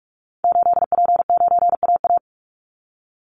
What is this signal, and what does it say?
Morse code, heard as a keyed tone, about 44 wpm, 700 Hz. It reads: 8P9AA
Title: Call Signs - 45wpm